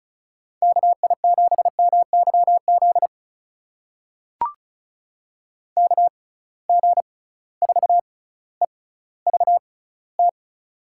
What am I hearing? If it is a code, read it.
Morse code, heard as a keyed tone, about 34 wpm, spaced as for 18 wpm, 700 Hz. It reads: KI7MYZ KG4EVT
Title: USA Call Signs - 6 Characters - 6x Character Spacing - 35wpm